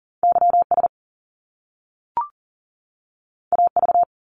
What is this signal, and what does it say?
Morse code, heard as a keyed tone, about 40 wpm, 700 Hz. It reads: YS AV
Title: Top Two Letter Combinations in Words 40wpm